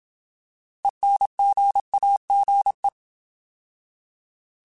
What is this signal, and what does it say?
Morse code, heard as a keyed tone, about 27 wpm, 800 Hz. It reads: ENGAGE
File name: Alphabetical / Alphabetical mp3s / Es @25